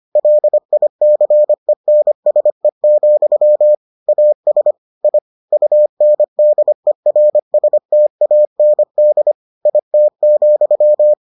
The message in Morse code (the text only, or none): LICENSE, AS I UNDERSTAND IT,